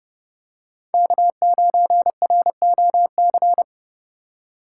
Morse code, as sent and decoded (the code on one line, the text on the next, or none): -.- ----. .-. --- -.-.
K9ROC